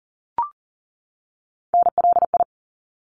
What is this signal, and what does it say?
Morse code, heard as a keyed tone, about 40 wpm, 700 Hz. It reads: NLI
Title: ARRL Sections 40wpm